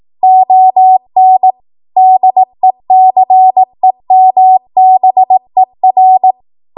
Morse code, heard as a keyed tone, about 18 wpm, 750 Hz. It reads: ON DECEMBER
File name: Archive / 18 WPM / 18WPM